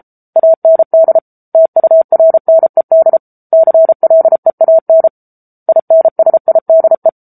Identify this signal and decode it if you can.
Morse code, heard as a keyed tone, about 33 wpm, 650 Hz. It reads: AND TURNED CLEAN INSIDE
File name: Book Two / 07-Chapter 7 - 33 WPM